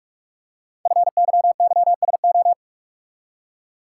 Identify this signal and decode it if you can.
Morse code, heard as a keyed tone, about 44 wpm, 700 Hz. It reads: FYYSO
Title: Code Groups - 5 Letters - 45wpm